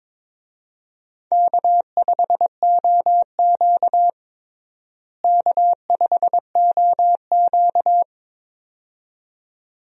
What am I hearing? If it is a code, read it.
Morse code, heard as a keyed tone, about 22 wpm, 700 Hz. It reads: K5OQ K5OQ
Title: USA Call Signs - 4 Characters - Repeated 2x - 22wpm